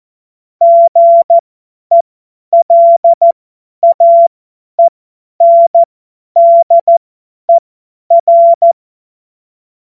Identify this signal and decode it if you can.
Morse code, heard as a keyed tone, about 14 wpm, spaced as for 10 wpm, 680 Hz. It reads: GELAENDER
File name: Morsekurs / Witze / Witz 13. Juli 2021, 50 F70